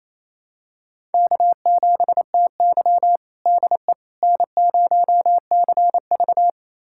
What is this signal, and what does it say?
Morse code, heard as a keyed tone, about 27 wpm, 700 Hz. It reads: K7TY DE N0CV
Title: USA Callsign DE Callsign - 4 Characters - 28wpm